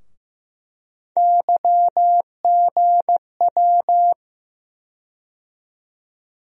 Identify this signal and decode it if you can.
Morse code, heard as a keyed tone, about 15 wpm, 700 Hz. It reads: YGW
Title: USA Call Sign Postfixes - 3 Letters 15wpm